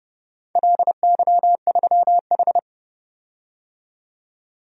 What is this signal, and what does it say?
Morse code, heard as a keyed tone, about 30 wpm, 700 Hz. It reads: LY3H
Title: DXCC-100 Call Signs - 4 Characters - 30wpm